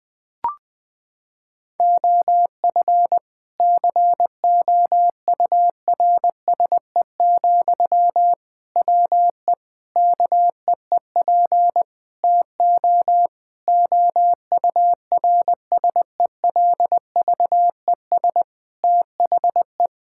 Morse code, heard as a keyed tone, about 20 wpm, 700 Hz. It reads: OF COURSE, WE KEEP TO OURSELVES THE
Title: Sentences from Top 1600 Words 20wpm